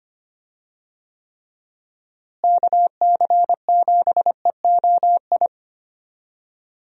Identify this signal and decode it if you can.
Morse code, heard as a keyed tone, about 25 wpm, 700 Hz. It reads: KC7EOI
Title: USA Call Signs - 6 Characters - Repeated 2x - 25wpm